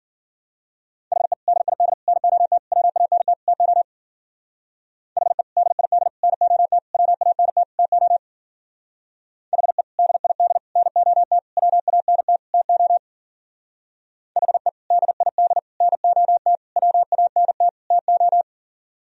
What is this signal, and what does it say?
Morse code, heard as a keyed tone, about 50 wpm, 700 Hz. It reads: HE DID NOT WANT TO HE DID NOT WANT TO HE DID NOT WANT TO HE DID NOT WANT TO
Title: Speed-Racing - Sets of 5 Words 40wpm